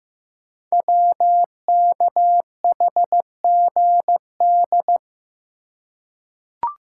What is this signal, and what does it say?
Morse code, heard as a keyed tone, about 15 wpm, 700 Hz. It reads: WKHGD
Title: Code Groups - 5 Letters - 15wpm